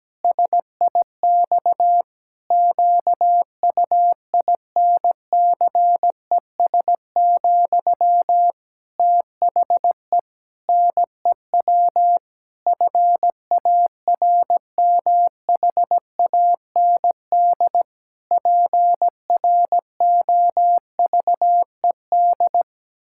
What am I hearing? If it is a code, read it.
Morse code, heard as a keyed tone, about 17 wpm, 700 Hz. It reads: SIX QUINCES, THE NEW FARMHAND PROVED
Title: Pangram Challenge v2 - 17wpm